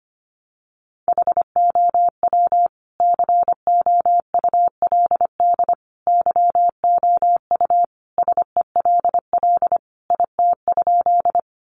Morse code, heard as a keyed tone, about 25 wpm, 700 Hz. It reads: HOW COULD YOU SELL IT?
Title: Sentences from Top 1100 Words 25wpm